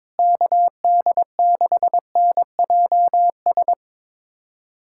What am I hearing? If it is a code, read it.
Morse code, heard as a keyed tone, about 22 wpm, 700 Hz. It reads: KD6NJS
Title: USA Callsign DE Callsign 22wpm